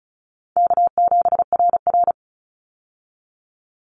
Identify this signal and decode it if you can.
Morse code, heard as a keyed tone, about 34 wpm, 700 Hz. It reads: K7RR
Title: Call Signs Repeated 3x - 35wpm